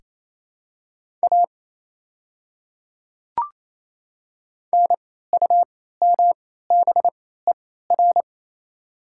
Morse code, heard as a keyed tone, about 28 wpm, spaced as for 16 wpm, 700 Hz. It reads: A NUMBER
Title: 3x Character Spacing - Top 300 Words - 28wpm